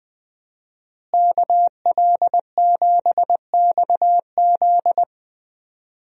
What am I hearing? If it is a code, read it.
Morse code, heard as a keyed tone, about 20 wpm, 700 Hz. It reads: KL7XZ
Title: DXCC-100 Call Signs - 5 Characters - Repeated 3x - 20wpm